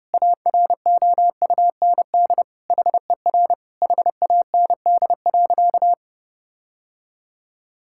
Morse code, heard as a keyed tone, about 30 wpm, 700 Hz. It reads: AROUND HER HAND.
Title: Sentences from Top 1600 Words 30wpm